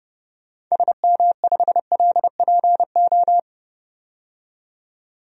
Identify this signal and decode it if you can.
Morse code, heard as a keyed tone, about 32 wpm, 700 Hz. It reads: SM5LPO
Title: DXCC-100 Call Signs - 6 Characters - 30wpm